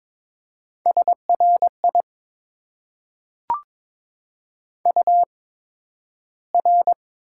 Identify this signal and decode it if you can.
Morse code, heard as a keyed tone, about 22 wpm, spaced as for 12 wpm, 700 Hz. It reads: SRI UR
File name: Top 90 QSO Elements - 8x Character Spacing - 22wpm